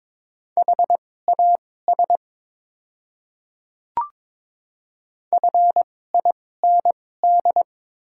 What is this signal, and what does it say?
Morse code, heard as a keyed tone, about 22 wpm, spaced as for 16 wpm, 700 Hz. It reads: HAS FIND
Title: 2x Character Spacing - Top 100 Words - 22wpm